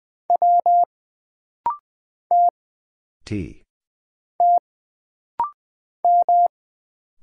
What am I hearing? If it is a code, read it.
Morse code, heard as a keyed tone, about 20 wpm, 700 Hz. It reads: W T T M
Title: Single Letter-Number Alt - Rapid-Fire 20wpm